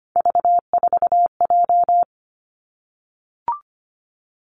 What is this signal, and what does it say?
Morse code, heard as a keyed tone, about 25 wpm, 700 Hz. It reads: V4J
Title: Three Challenging Letter-Number Combinations 25wpm